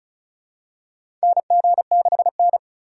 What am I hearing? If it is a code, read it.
Morse code, heard as a keyed tone, about 36 wpm, 700 Hz. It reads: NG6N
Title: USA Call Signs - 4 Characters - 4x Character Spacing - 35wpm